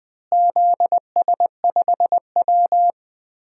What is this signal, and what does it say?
Morse code, heard as a keyed tone, about 20 wpm, 700 Hz. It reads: ZS5W
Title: DXCC-100 Call Signs - 4 Characters - Repeated 2x - 20wpm